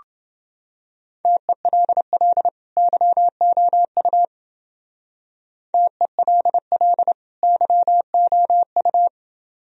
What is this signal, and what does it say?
Morse code, heard as a keyed tone, about 28 wpm, 700 Hz. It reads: TELL YOU TELL YOU